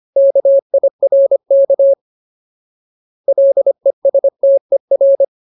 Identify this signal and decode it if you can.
Morse code, heard as a keyed tone, about 25 wpm, 550 Hz. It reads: KIRK LESTER
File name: Ultimate Word List / 25WPM Ultimate